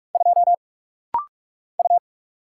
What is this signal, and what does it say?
Morse code, heard as a keyed tone, about 46 wpm, 700 Hz. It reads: 2 U
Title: Single Letter-Number Alt - Rapid-Fire 45wpm